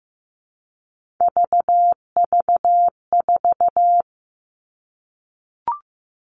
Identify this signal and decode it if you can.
Morse code, heard as a keyed tone, about 15 wpm, 700 Hz. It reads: VV4
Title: All Cut Numbers - 3-Digit 15wpm